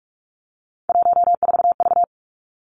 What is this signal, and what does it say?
Morse code, heard as a keyed tone, about 46 wpm, 700 Hz. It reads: 14V
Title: Three Challenging Letter-Number Combinations 45wpm